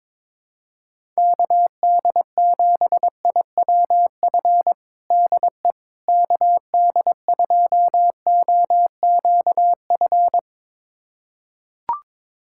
Morse code, heard as a keyed tone, about 22 wpm, 700 Hz. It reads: KD7IWF DE KD2OQF
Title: USA Callsign DE Callsign - 6 Characters - 22wpm